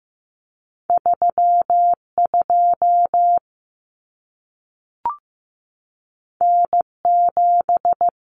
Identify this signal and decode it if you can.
Morse code, heard as a keyed tone, about 15 wpm, 700 Hz. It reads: 32 N7
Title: Common Cut Numbers - 2-Digit 15wpm